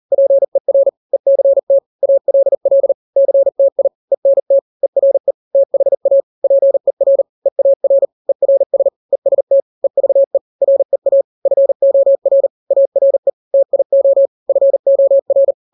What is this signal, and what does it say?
Morse code, heard as a keyed tone, about 40 wpm, 550 Hz. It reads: PER ECT ALL CTI ENT ERE THA PER EAR ERS EST EVE REA FOR ARE TIO FOR
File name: n-grams / 3-grams 40 35 WPM